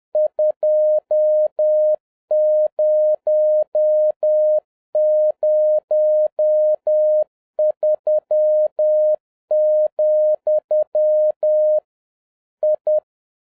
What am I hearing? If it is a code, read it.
Morse code, heard as a keyed tone, about 10 wpm, 610 Hz. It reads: 2003, I
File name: 2011-07-mp3 / n41 EXCITING QRP CW OPERATION 10wpm